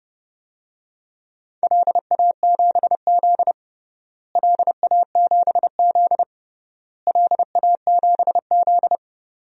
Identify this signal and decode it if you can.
Morse code, heard as a keyed tone, about 30 wpm, 700 Hz. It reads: LA7Z LA7Z LA7Z
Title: DXCC-100 Call Signs - 4 Characters - Repeated 3x - 30wpm